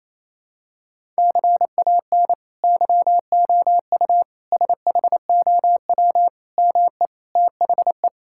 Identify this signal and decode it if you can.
Morse code, heard as a keyed tone, about 28 wpm, 700 Hz. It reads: CAN YOU SHOW ME THE
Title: Sentences from Top 1000 Words - Review Part 6 - 28wpm